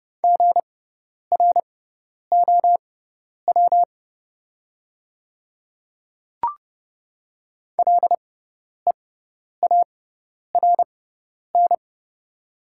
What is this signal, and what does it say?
Morse code, heard as a keyed tone, about 30 wpm, spaced as for 10 wpm, 690 Hz. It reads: GROW LEARN
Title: 6x Character Spacing - Top 300 Words - 30wpm